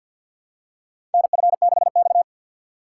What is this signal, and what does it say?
Morse code, heard as a keyed tone, about 52 wpm, 700 Hz. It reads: NF6X
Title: USA Call Signs - 4 Characters - 2x Character Spacing - 50wpm